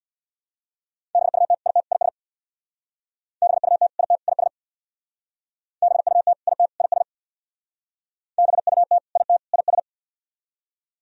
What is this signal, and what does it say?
Morse code, heard as a keyed tone, about 63 wpm, 700 Hz. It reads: BUT IT IS BUT IT IS BUT IT IS BUT IT IS